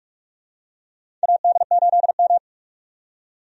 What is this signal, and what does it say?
Morse code, heard as a keyed tone, about 46 wpm, 700 Hz. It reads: AD8M
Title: USA Call Signs - 4 Characters - 45wpm